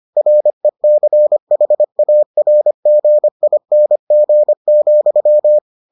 Morse code, heard as a keyed tone, about 25 wpm, 600 Hz. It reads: RECHARGING,